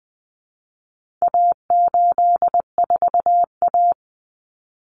Morse code, heard as a keyed tone, about 20 wpm, 700 Hz. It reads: A84A
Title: Common Cut Numbers - 4-Digit 20wpm